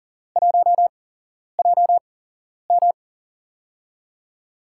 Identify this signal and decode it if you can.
Morse code, heard as a keyed tone, about 41 wpm, spaced as for 11 wpm, 700 Hz. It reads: 1JM
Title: USA Call Signs - 8x Character Spacing - 40wpm